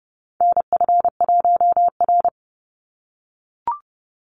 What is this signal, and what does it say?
Morse code, heard as a keyed tone, about 30 wpm, 700 Hz. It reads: NF1R
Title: NA SOTA Chaser Callsigns 30wpm